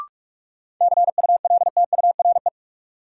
Morse code, heard as a keyed tone, about 42 wpm, 700 Hz. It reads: CULTURE
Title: Sentences from Top 1000 Words - Review Part 6 - New Word Drill - 45wpm